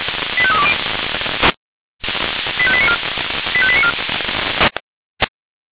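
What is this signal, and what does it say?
Five-tone selective call (ZVEI), heard as quick runs of five tones, tones of about 70 ms. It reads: 85320 85894 85894
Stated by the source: ZVEI five-tone selective call received by SDR.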